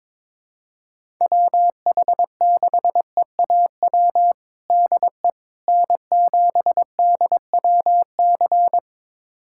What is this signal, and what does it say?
Morse code, heard as a keyed tone, about 22 wpm, 700 Hz. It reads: WH6EAW DE N7DWC
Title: USA Callsign DE Callsign 22wpm